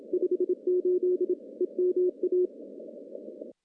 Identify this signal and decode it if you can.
Morse code, heard as a keyed tone, about 27 wpm, 360 Hz. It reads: 58 EMA
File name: contests / CW-SS-k5tr- -0100